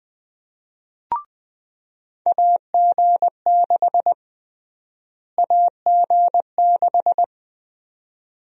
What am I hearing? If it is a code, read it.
Morse code, heard as a keyed tone, about 20 wpm, 700 Hz. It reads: AG6 AG6
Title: USA Call Sign Prefixes - Repeated 2x - 20wpm